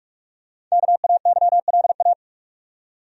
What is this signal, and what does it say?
Morse code, heard as a keyed tone, about 46 wpm, 700 Hz. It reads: KAYLA
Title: USA Common Names - 45wpm